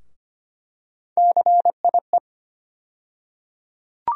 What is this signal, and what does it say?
Morse code, heard as a keyed tone, about 25 wpm, 700 Hz. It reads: CIE
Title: USA Call Sign Postfixes - 3 Letters 25wpm